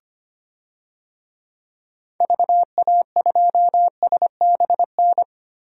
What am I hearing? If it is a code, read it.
Morse code, heard as a keyed tone, about 25 wpm, 700 Hz. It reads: VA2SBN
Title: DXCC-100 Call Signs - 6 Characters - Repeated 2x - 25wpm